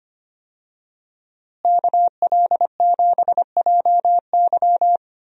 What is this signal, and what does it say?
Morse code, heard as a keyed tone, about 25 wpm, 700 Hz. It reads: KL7JY
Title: DXCC-100 Call Signs - 5 Characters - Repeated 2x - 25wpm